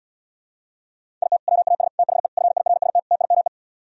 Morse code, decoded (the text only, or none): IT WAS THE FIRST TIME